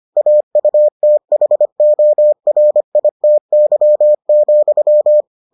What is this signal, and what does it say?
Morse code, heard as a keyed tone, about 25 wpm, 600 Hz. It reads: AUTHORITY,